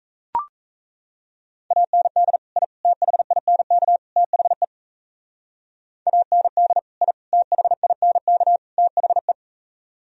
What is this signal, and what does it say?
Morse code, heard as a keyed tone, about 36 wpm, 700 Hz. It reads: AND I THINK THE AND I THINK THE